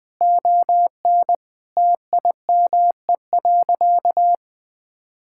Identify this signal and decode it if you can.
Morse code, heard as a keyed tone, about 21 wpm, 700 Hz. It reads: ON TIME.
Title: Sentences with 2-Word Phrases - 20wpm